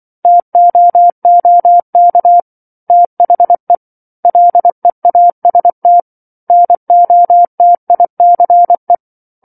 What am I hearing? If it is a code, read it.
Morse code, heard as a keyed tone, about 24 wpm, 700 Hz. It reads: TOOK THE LEAST NOTICE
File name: 07-Chapter-07 24 WPM